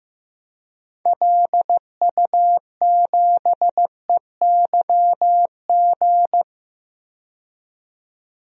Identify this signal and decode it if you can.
Morse code, heard as a keyed tone, about 15 wpm, 700 Hz. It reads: LU7EYG
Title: DXCC-100 Call Signs - 6 Characters - 15wpm